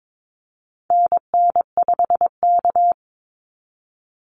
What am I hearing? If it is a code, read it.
Morse code, heard as a keyed tone, about 22 wpm, 700 Hz. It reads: NN5K
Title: NA SOTA Activator Callsigns 22wpm